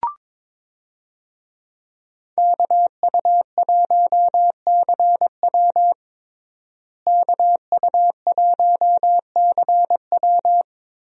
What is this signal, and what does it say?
Morse code, heard as a keyed tone, about 21 wpm, 700 Hz. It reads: KU1CW KU1CW
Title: USA Call Signs - 5 Characters - Repeated 3x - 22wpm